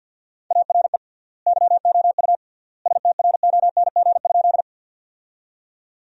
Morse code, heard as a keyed tone, about 50 wpm, 700 Hz. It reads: ARE YOU STRONG?
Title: 3x Word-Spacing - Sentences from Top 500 Words - 50wpm